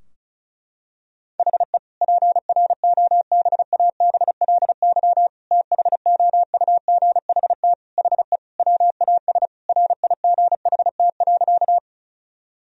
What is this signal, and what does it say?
Morse code, heard as a keyed tone, about 36 wpm, 700 Hz. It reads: HE PROBABLY THOUGHT HE WAS RIGHT.
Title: Sentences from Top 1000 Words - Review Part 4 - 35wpm